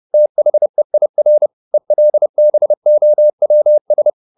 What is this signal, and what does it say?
Morse code, heard as a keyed tone, about 30 wpm, 600 Hz. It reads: THEIR ELBOWS
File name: the old man and the sea / the old man and the sea 1 30wpm